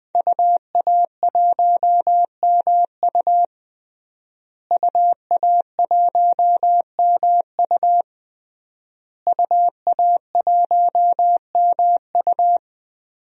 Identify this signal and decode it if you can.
Morse code, heard as a keyed tone, about 21 wpm, 700 Hz. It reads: UA1MU UA1MU UA1MU
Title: DXCC-100 Call Signs - 5 Characters - Repeated 3x - 20wpm